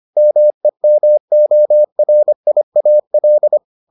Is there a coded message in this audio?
Yes: Morse code from 0.2 to 3.6 s, heard as a keyed tone, about 25 wpm, 600 Hz. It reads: MEMORIAL